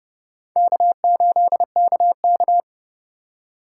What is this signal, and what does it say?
Morse code, heard as a keyed tone, about 37 wpm, 700 Hz. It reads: K8KK
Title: Speed Racing - USA 1x2 Call Signs 30wpm